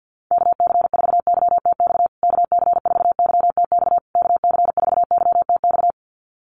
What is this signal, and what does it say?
Morse code, heard as a keyed tone, about 50 wpm, 700 Hz. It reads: KC4YTX KC4YTX KC4YTX
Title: USA Call Signs Repeated 3x 50wpm